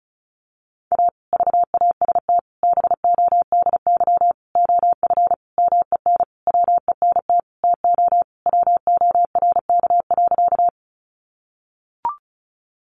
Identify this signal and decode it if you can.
Morse code, heard as a keyed tone, about 34 wpm, 700 Hz. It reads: A VAST BODY OF MEN WENT TO WORK.
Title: Sentences from Top 1300 Words 35wpm